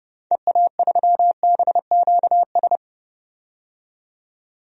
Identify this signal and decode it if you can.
Morse code, heard as a keyed tone, about 30 wpm, 700 Hz. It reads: EA3BQS
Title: DXCC-100 Call Signs - 6 Characters - 30wpm